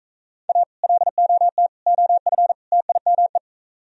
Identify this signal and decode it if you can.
Morse code, heard as a keyed tone, about 43 wpm, 700 Hz. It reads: A LOT OF TIME